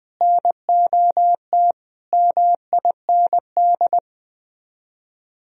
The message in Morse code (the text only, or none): NOT MIND